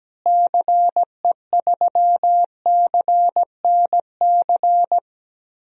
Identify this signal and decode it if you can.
Morse code, heard as a keyed tone, about 17 wpm, 700 Hz. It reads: CE3CNC
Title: DXCC-100 Call Signs - 6 Characters - Repeated 3x - 17wpm